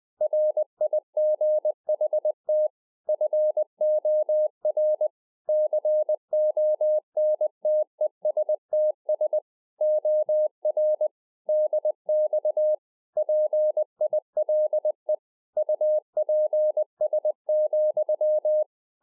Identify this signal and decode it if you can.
Morse code, heard as a keyed tone, about 20 wpm, 610 Hz. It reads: RIGHT FOR CONTESTS OR DX PILE UPS,
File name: 2011-04-mp3 / A SMALL TRANCEIVER 3 20WPM